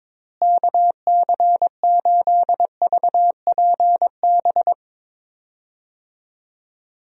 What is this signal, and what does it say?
Morse code, heard as a keyed tone, about 22 wpm, 700 Hz. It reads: KC8VPB
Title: USA Call Signs - 6 Characters - Repeated 2x - 22wpm